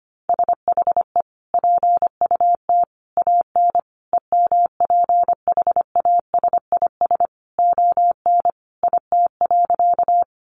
Morse code, heard as a keyed tone, about 25 wpm, 700 Hz. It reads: SHE PUT AN EMPHASIS ON IT.
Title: Sentences from Top 1400 Words 25wpm